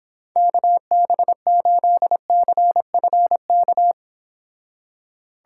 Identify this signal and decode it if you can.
Morse code, heard as a keyed tone, about 26 wpm, 700 Hz. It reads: KB8CFK